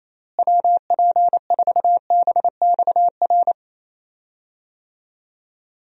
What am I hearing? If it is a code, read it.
Morse code, heard as a keyed tone, about 28 wpm, 700 Hz. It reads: WP4BXR
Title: USA Call Signs - 6 Characters - Repeated 2x - 28wpm